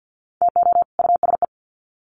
Morse code, heard as a keyed tone, about 52 wpm, 700 Hz. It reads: TO USE
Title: Sets of 2 Words 50wpm